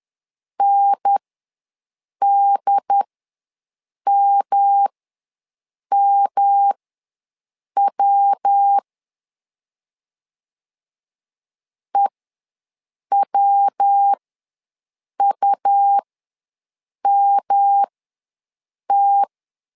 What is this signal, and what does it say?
Morse code, heard as a keyed tone, about 10 wpm, spaced as for 6 wpm, 800 Hz. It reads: NDMMW EWUMT
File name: LEZIONI / 3 MWRO / 3 MW ws 10wpm 2213